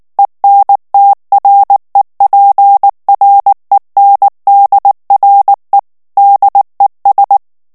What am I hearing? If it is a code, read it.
Morse code, heard as a keyed tone, about 19 wpm, 800 Hz. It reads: ENTREPRENDRE DES
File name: morse mp3 / FRANCAIS TEXTE CLAIR 1 VITESSE 18 MOTS MINUTES